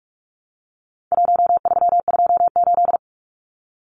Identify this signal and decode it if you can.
Morse code, heard as a keyed tone, about 44 wpm, 700 Hz. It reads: 1328
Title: Numbers - Contest Style - 4-Digit 45wpm